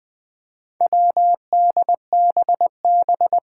Morse code, heard as a keyed tone, about 20 wpm, 700 Hz. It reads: WDBB
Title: Four Challenging Letter-Number Combinations 20wpm